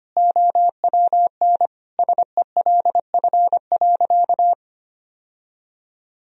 Morse code, heard as a keyed tone, about 25 wpm, 700 Hz. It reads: OWN SELF.
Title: Sentences from Top 1000 Words - Review Part 1 - 25wpm